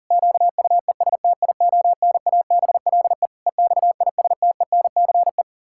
Morse code, heard as a keyed tone, about 37 wpm, 700 Hz. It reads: QUESTIONABLE EXISTENCE
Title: Pangram Challenge v2 - 40wpm